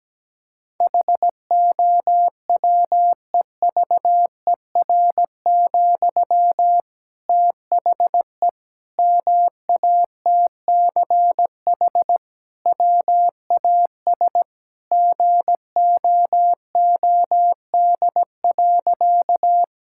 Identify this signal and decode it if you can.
Morse code, heard as a keyed tone, about 17 wpm, 700 Hz. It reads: HOWEVER, THE MATCH WAS GOOD.
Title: Sentences from Top 1600 Words 17wpm